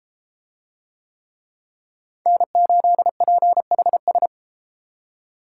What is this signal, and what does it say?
Morse code, heard as a keyed tone, about 31 wpm, 700 Hz. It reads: N8PHS